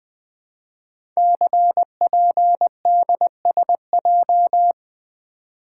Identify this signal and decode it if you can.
Morse code, heard as a keyed tone, about 20 wpm, 700 Hz. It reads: CPDSJ
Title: Code Groups - 5 Letter:Numbers - 20wpm